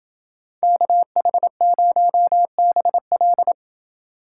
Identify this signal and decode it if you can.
Morse code, heard as a keyed tone, about 28 wpm, 700 Hz. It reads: KH0BL